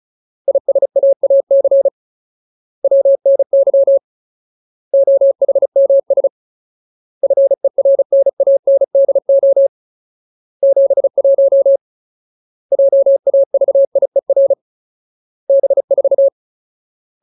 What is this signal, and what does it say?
Morse code, heard as a keyed tone, about 34 wpm, 550 Hz. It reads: ISAAC WNY OHMS FERNANDO 71 JAVIER B4
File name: Ultimate Word List / 35WPM Ultimate